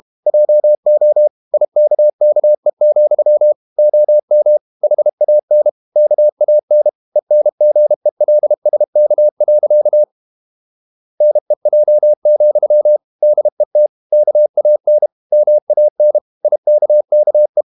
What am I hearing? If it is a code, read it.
Morse code, heard as a keyed tone, about 32 wpm, 600 Hz. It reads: JO IKKE, OM HAN KAN ENGELSK. NEJ, DET KAN MAN IKKE
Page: Ferie 32 wpm